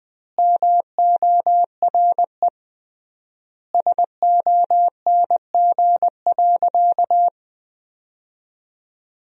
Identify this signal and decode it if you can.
Morse code, heard as a keyed tone, about 20 wpm, 700 Hz. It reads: MORE SONG.
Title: 3x Word-Spacing - Sentences from Top 400 Words - 20wpm